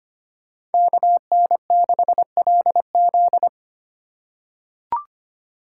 Morse code, heard as a keyed tone, about 25 wpm, 700 Hz. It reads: KN6LZ
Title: USA Call Signs - 5 Characters - 2x Character Spacing - 25wpm